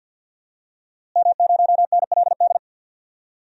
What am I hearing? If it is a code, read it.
Morse code, heard as a keyed tone, about 46 wpm, 700 Hz. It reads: M0NLD
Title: DXCC-100 Call Signs - 5 Characters - Repeated 2x - 50wpm